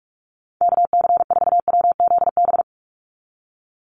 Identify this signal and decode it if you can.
Morse code, heard as a keyed tone, about 44 wpm, 700 Hz. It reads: KC4WZB
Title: NA SOTA Activator Callsigns 45wpm